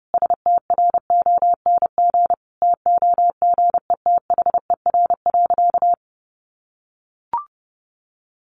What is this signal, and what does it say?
Morse code, heard as a keyed tone, about 30 wpm, 700 Hz. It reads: STRONG TOGETHER.
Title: Sentences from Top 500 Words 30wpm